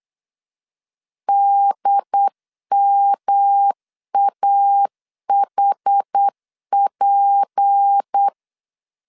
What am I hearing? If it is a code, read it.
Morse code, heard as a keyed tone, about 8 wpm, 800 Hz. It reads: DMAHP
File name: CW / DET / 1 9 1500C S C 8wpm 5653